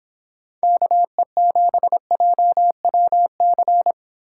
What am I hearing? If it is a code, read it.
Morse code, heard as a keyed tone, about 31 wpm, 700 Hz. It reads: KE7JWC